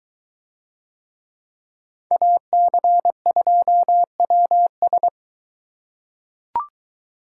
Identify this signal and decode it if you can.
Morse code, heard as a keyed tone, about 23 wpm, 700 Hz. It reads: AC2WS